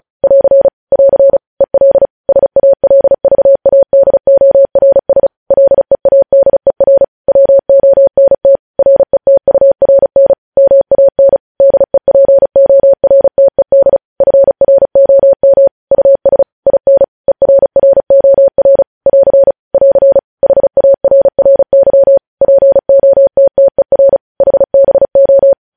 Morse code, heard as a keyed tone, about 36 wpm, 550 Hz. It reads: <AR> <AR> EL SALVADORS LEADER WONT RETURN MAN DEPORTED FROM US IN ERROR <AR> <AR> HARRY POTTER HBO
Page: news headlines in morse code at 35 wpm